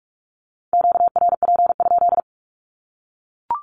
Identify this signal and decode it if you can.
Morse code, heard as a keyed tone, about 44 wpm, 700 Hz. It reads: QRP?
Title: Q-Codes 45wpm